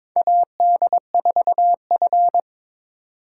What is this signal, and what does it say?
Morse code, heard as a keyed tone, about 22 wpm, 700 Hz. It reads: AD4F
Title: USA Call Signs - 4 Characters - 2x Character Spacing - 22wpm